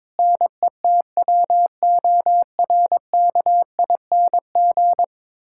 Morse code, heard as a keyed tone, about 22 wpm, 700 Hz. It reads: NETWORKING
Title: Practice QSO - Exchange 3 - More Info - 22wpm